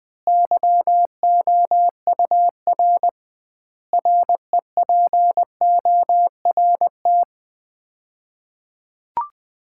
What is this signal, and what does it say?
Morse code, heard as a keyed tone, about 21 wpm, 700 Hz. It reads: YOUR REPORT
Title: Common QSO Phrases - 2x Word-Spacing - 20wpm